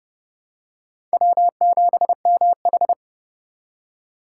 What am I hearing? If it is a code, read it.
Morse code, heard as a keyed tone, about 30 wpm, 700 Hz. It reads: W7MH
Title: USA Call Signs - 4 Characters - 3x Character Spacing - 30wpm